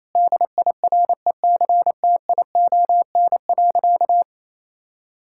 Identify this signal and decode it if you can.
Morse code, heard as a keyed tone, about 28 wpm, 700 Hz. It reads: DIRECTION.
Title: Sentences from Top 1000 Words - Review Part 5 - 28wpm